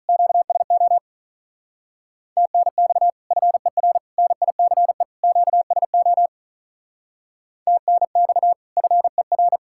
Morse code, heard as a keyed tone, about 43 wpm, 700 Hz. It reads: QSO TNX FER NICE QSO TNX FER
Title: Speed-Racing Common QSO Phrases - 35wpm